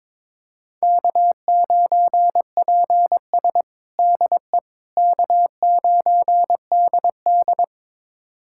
Morse code, heard as a keyed tone, about 22 wpm, 700 Hz. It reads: K9PS DE K9DD
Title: USA Callsign DE Callsign - 4 Characters - 22wpm